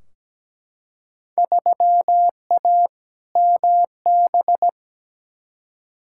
Morse code, heard as a keyed tone, about 17 wpm, 700 Hz. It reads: 3A MB
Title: Field Day Exchanges Encore - 17wpm